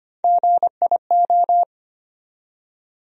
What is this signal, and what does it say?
Morse code, heard as a keyed tone, about 25 wpm, 700 Hz. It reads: GIO
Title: USA Call Sign Postfixes - 3 Letters 25wpm